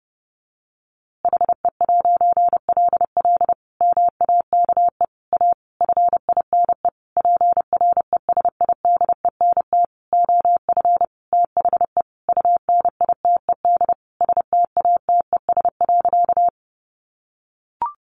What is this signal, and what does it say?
Morse code, heard as a keyed tone, about 30 wpm, 700 Hz. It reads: HE'LL MAKE A FINE PRESIDENT OF THE UNITED STATES.
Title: A Sound Of Thunder 30wpm